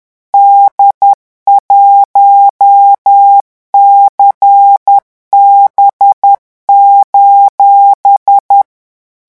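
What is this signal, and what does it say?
Morse code, heard as a keyed tone, about 10 wpm, 800 Hz. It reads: D1CB:
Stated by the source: French Morse code revision lesson 18 at 10 words per minute.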